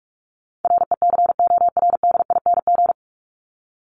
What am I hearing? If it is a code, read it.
Morse code, heard as a keyed tone, about 44 wpm, 700 Hz. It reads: RECORDING
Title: Top 1501-1750 Words 45wpm